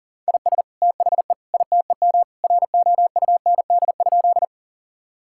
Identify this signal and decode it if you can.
Morse code, heard as a keyed tone, about 37 wpm, 700 Hz. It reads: IS THE ITEM ROUND?
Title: Sentences from Top 1000 Words - Review Part 6 - 40wpm